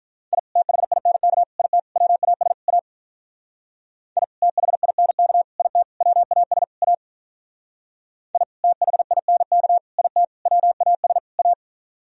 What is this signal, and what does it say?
Morse code, heard as a keyed tone, about 42 wpm, 700 Hz. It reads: I THINK IT WAS A I THINK IT WAS A I THINK IT WAS A
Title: Speed-Racing - Sets of 5 Words 35wpm